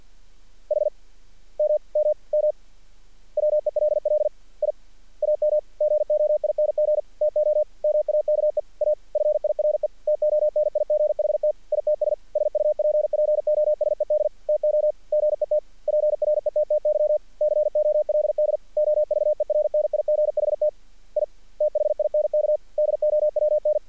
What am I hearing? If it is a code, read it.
Morse code, heard as a keyed tone, about 48 wpm, 600 Hz. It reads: HMMM WELL I AM GOING TO MAKE A FIRE TONIGHT ITS SUPPOSED TO GET PRETTY COLD OVERNIGHT I THINK DOWN
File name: archived / CRSnet